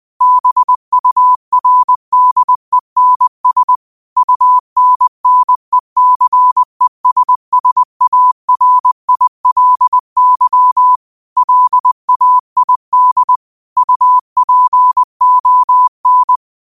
Morse code, heard as a keyed tone, about 20 wpm, 1000 Hz. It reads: BURDENS UNNECESSARILY LAID UPON
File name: qotdfiles / 20wpm